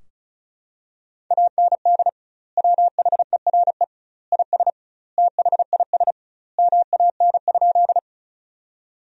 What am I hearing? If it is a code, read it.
Morse code, heard as a keyed tone, about 34 wpm, 700 Hz. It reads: AND WHERE IS THIS MAN?
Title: 2x Word-Spacing - Sentences from Top 200 Words - 35wpm